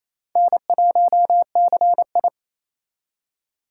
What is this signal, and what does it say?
Morse code, heard as a keyed tone, about 36 wpm, 700 Hz. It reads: N1CI